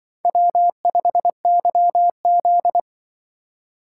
Morse code, heard as a keyed tone, about 26 wpm, 700 Hz. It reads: W5YZ